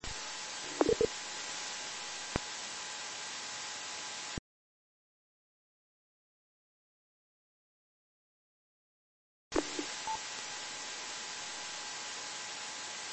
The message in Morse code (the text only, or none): none